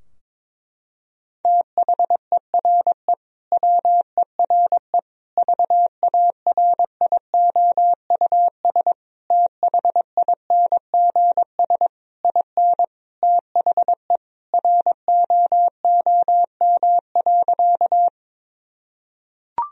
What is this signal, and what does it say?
Morse code, heard as a keyed tone, about 22 wpm, 700 Hz. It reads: THERE WERE VARIOUS THINGS IN THE ROOM.
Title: Sentences from Top 1000 Words - Review Part 4 - 22wpm